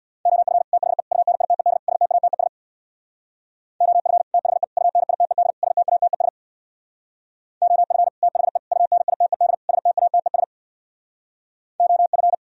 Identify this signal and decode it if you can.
Morse code, heard as a keyed tone, about 64 wpm, 700 Hz. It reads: OF THE UNITED STATES OF THE UNITED STATES OF THE UNITED STATES OF